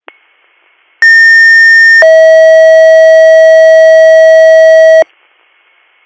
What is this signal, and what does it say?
Two-tone page, heard as a one-second tone, then a three-second tone, about 1800 then 640 Hz. It is heard from 1.0 to 5.0 s.